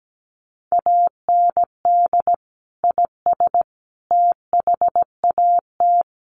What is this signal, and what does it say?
Morse code, heard as a keyed tone, about 17 wpm, 700 Hz. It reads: AND IS THAT
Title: Sentences from Top 100 Words 17wpm